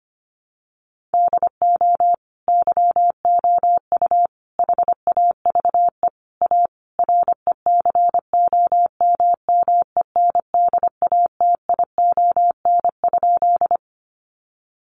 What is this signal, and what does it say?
Morse code, heard as a keyed tone, about 25 wpm, 700 Hz. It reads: DO YOU HAVE A RECOMMENDATION?
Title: Sentences from Top 1400 Words 25wpm